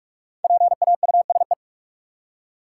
Morse code, heard as a keyed tone, about 41 wpm, 700 Hz. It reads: PAUSE
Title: Five Letter Words - 45wpm